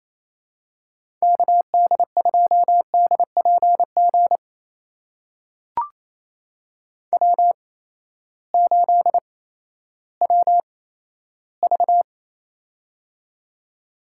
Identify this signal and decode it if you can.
Morse code, heard as a keyed tone, about 28 wpm, spaced as for 14 wpm, 700 Hz. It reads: KD2DPG W8WV
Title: USA Call Signs - 8x Character Spacing - 28wpm